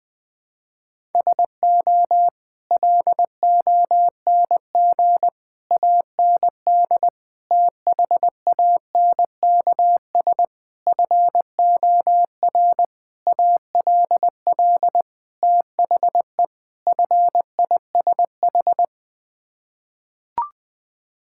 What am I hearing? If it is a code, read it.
Morse code, heard as a keyed tone, about 20 wpm, 700 Hz. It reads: SO LONG AND THANKS FOR ALL THE FISH
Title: Sets of 8 Words - 20wpm